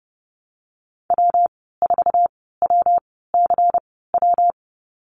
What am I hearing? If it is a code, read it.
Morse code, heard as a keyed tone, about 30 wpm, spaced as for 17 wpm, 700 Hz. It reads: W4WCW
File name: Lesson 19g - Letter w - Call Signs 30wpm